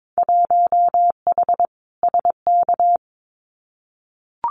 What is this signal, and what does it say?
Morse code, heard as a keyed tone, about 22 wpm, 700 Hz. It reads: 1H SK
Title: Winter Field Day Exchanges 22wpm